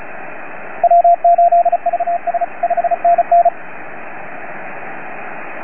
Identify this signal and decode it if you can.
Morse code, heard as a keyed tone, about 36 wpm, 690 Hz, with radio noise. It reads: W8VS 5NN